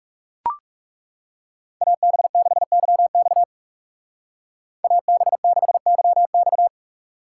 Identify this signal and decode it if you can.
Morse code, heard as a keyed tone, about 39 wpm, 700 Hz. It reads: AB6YX AB6YX